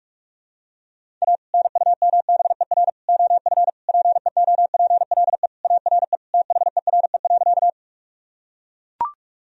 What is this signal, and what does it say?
Morse code, heard as a keyed tone, about 41 wpm, 700 Hz. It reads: A NUMBER OF PEOPLE ARE THERE.
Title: Sentences from Top 1000 Words - Review Part 1 - 45wpm